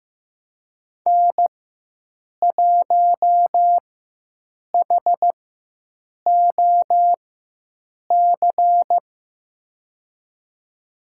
Morse code, heard as a keyed tone, about 15 wpm, spaced as for 7 wpm, 700 Hz. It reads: N1HOC
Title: USA Call Signs - 5 Characters - 4x Character Spacing - 15wpm